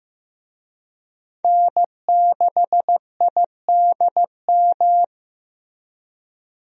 Morse code, heard as a keyed tone, about 15 wpm, 700 Hz. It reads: N6IDM
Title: USA Call Signs - 5 Characters - Repeated 3x - 15wpm